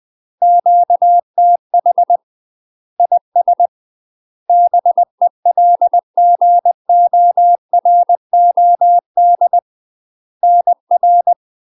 Morse code, heard as a keyed tone, about 20 wpm, 700 Hz. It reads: QTH IS BELGOROD NR
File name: UK MASTER QSOs TRAINING / 20WPM / UK MASTER QSOs 0020